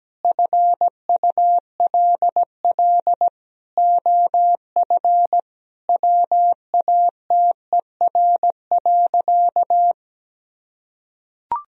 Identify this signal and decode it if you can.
Morse code, heard as a keyed tone, about 17 wpm, 700 Hz. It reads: FULL OF WATER.
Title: Sentences from Top 1000 Words - Review Part 3 - 17wpm